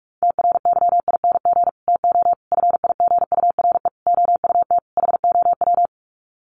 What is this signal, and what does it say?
Morse code, heard as a keyed tone, about 44 wpm, 700 Hz. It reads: TRYING TO FIGURE OUT HOW
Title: Sets of 5 Words Encore 45wpm